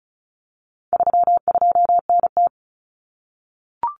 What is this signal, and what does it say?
Morse code, heard as a keyed tone, about 34 wpm, 700 Hz. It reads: 32NT
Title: Common Cut Numbers - 4-Digit 35wpm